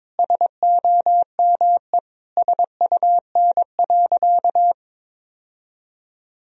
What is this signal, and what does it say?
Morse code, heard as a keyed tone, about 22 wpm, 700 Hz. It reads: SOME SUN.
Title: Sentences from Top 1000 Words - Review Part 1 - 22wpm